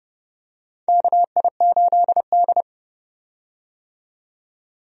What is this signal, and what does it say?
Morse code, heard as a keyed tone, about 32 wpm, 700 Hz. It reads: KI8D
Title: USA Call Signs - 4 Characters - 30wpm